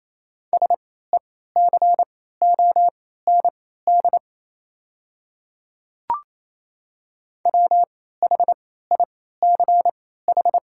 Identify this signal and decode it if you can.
Morse code, heard as a keyed tone, about 28 wpm, spaced as for 16 wpm, 700 Hz. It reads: SECOND WHICH
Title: 3x Character Spacing - Top 300 Words - 28wpm